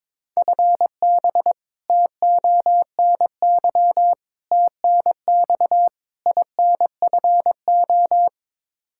Morse code, heard as a keyed tone, about 22 wpm, 700 Hz. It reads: FB TONY TNX INFO
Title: Practice QSO - Exchange 3 - More Info - 22wpm